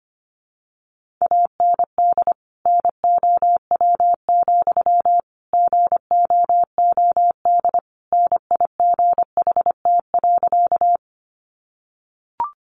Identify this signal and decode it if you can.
Morse code, heard as a keyed tone, about 25 wpm, 700 Hz. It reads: AND NOW, GOOD NIGHT.
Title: Sentences from Top 300 Words 25wpm